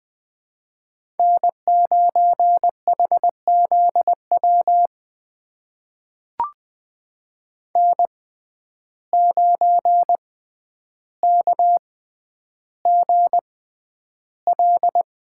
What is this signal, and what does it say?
Morse code, heard as a keyed tone, about 21 wpm, spaced as for 10 wpm, 700 Hz. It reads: N9HZW N9KGL
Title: USA Call Signs - 5 Characters - 6x Character Spacing - 20wpm